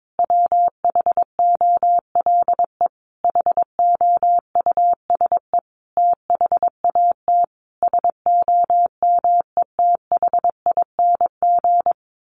Morse code, heard as a keyed tone, about 22 wpm, 700 Hz. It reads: WHOLE HOUSE THAT SOMETHING
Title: Sentences from Top 1200 Words 22wpm